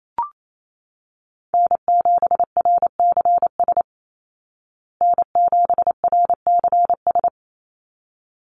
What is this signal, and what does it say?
Morse code, heard as a keyed tone, about 28 wpm, 700 Hz. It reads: N7RCS N7RCS
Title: Call Signs Repeated 3x - 28wpm